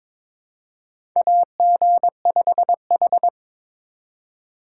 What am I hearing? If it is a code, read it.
Morse code, heard as a keyed tone, about 22 wpm, 700 Hz. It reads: AG5H
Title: USA Call Signs - 4 Characters - 4x Character Spacing - 22wpm